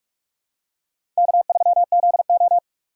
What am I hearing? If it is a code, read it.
Morse code, heard as a keyed tone, about 46 wpm, 700 Hz. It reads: K3ZO
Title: USA Call Signs - 4 Characters - 3x Character Spacing - 45wpm